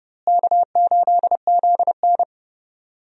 Morse code, heard as a keyed tone, about 30 wpm, 700 Hz. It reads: K8ZN
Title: USA Call Signs - 4 Characters - Repeated 2x - 30wpm